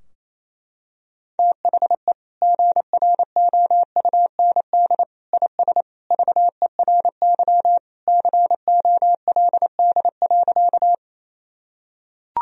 Morse code, heard as a keyed tone, about 28 wpm, 700 Hz. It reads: THE GROUND IS VERY COLD.
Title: Sentences from Top 1000 Words - Review Part 2 - 28wpm